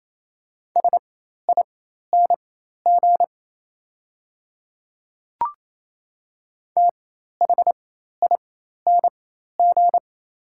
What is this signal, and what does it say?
Morse code, heard as a keyed tone, about 28 wpm, spaced as for 13 wpm, 690 Hz. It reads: SING THING
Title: 4x Character Spacing - Top 500 Words - 28wpm